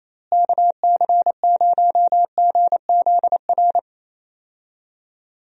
Morse code, heard as a keyed tone, about 27 wpm, 700 Hz. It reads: KC0GZR
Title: USA Call Signs - 6 Characters - Repeated 2x - 28wpm